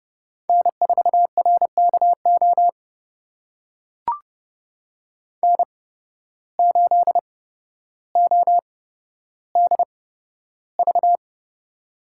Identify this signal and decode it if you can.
Morse code, heard as a keyed tone, about 29 wpm, spaced as for 13 wpm, 700 Hz. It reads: N4RKO N8ODV
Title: USA Call Signs - 5 Characters - 8x Character Spacing - 30wpm